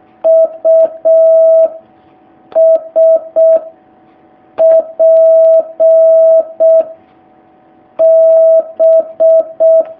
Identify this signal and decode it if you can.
Morse code, heard as a keyed tone, about 6 wpm, 650 Hz. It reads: U S P B
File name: jh8cbh / 2019cw